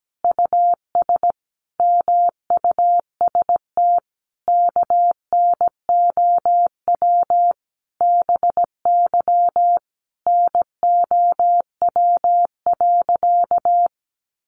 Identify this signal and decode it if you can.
Morse code, heard as a keyed tone, about 17 wpm, 700 Hz. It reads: US MUST KNOW BY NOW.
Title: Sentences from Top 1400 Words 17wpm